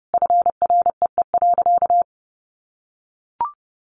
Morse code, heard as a keyed tone, about 28 wpm, 700 Hz. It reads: FREE.
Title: Sentences from Top 500 Words 30wpm